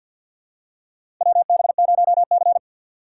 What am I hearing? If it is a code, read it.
Morse code, heard as a keyed tone, about 52 wpm, 700 Hz. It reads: WB0C
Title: USA Call Signs - 4x Character Spacing - 50wpm